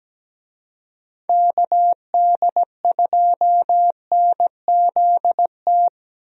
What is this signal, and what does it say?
Morse code, heard as a keyed tone, about 17 wpm, 700 Hz. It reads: KD2NZT
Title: USA Call Signs - 6 Characters - Repeated 2x - 17wpm